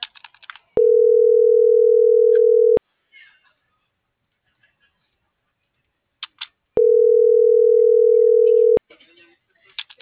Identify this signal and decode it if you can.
Ringback tone (North America), 440 + 480 Hz.